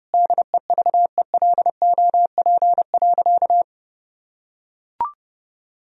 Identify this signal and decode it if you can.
Morse code, heard as a keyed tone, about 32 wpm, 700 Hz. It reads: DEVELOP.
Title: Sentences from Top 1000 Words - Review Part 2 - 30wpm